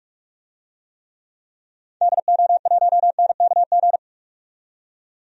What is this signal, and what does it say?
Morse code, heard as a keyed tone, about 46 wpm, 700 Hz. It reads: DO1NKG
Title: DXCC-100 Call Signs - 6 Characters - 45wpm